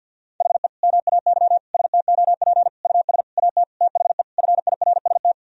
Morse code, heard as a keyed tone, about 46 wpm, 700 Hz. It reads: HE MAY STOP US AT THE FIRST
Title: Sentences from Top 1000 Words - Review Part 2 - 50wpm